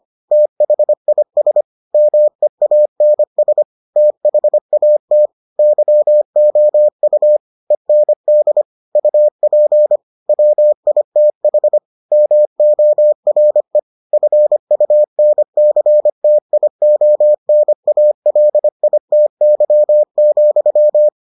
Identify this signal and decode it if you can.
Morse code, heard as a keyed tone, about 25 wpm, 600 Hz. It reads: THIS MEANS THAT YOU END UP WITH MORE FUNCTIONALITY,